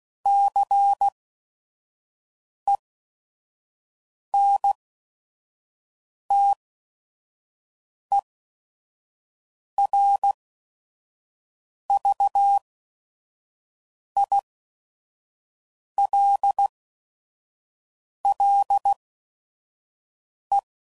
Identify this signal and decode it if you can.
Morse code, heard as a keyed tone, about 16 wpm, spaced as for 5 wpm, 800 Hz. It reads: CENTERVILLE
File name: CW Code Groups @5WPM / fromSM's / Cities